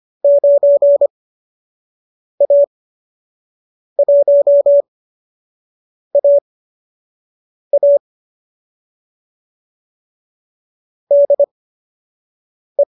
Morse code, heard as a keyed tone, about 25 wpm, spaced as for 6 wpm, 590 Hz. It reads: 9A1AA DE
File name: qso012 06